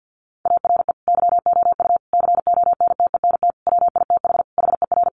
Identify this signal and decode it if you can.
Morse code, heard as a keyed tone, about 50 wpm, 700 Hz. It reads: ARE YOU CONTENT WITH HER
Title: Sentences from Top 900 Words 50wpm